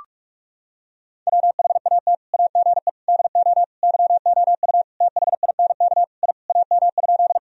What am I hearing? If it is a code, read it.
Morse code, heard as a keyed tone, about 47 wpm, 700 Hz. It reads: WHAT AGE DO YOU THINK I AM?
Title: Sentences from Top 1000 Words - Review Part 3 - 45wpm